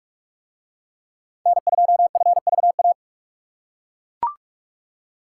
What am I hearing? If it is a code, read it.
Morse code, heard as a keyed tone, about 41 wpm, 700 Hz. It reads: N2FVA